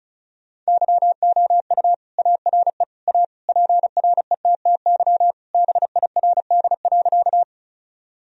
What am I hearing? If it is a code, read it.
Morse code, heard as a keyed tone, about 34 wpm, 700 Hz. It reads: YOU ARE A PRETTY BIRD.
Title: Sentences from Top 1000 Words - Review Part 2 - 35wpm